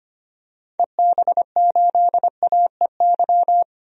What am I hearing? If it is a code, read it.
Morse code, heard as a keyed tone, about 25 wpm, 700 Hz. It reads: EB8AEY
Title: DXCC-100 Call Signs - 6 Characters - 25wpm